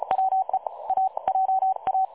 Morse code, heard as a keyed tone, about 33 wpm, 760 Hz. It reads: WE A JA
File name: dlazaawansowanych / nagraniazpasma / websdr recording 3565.0kHz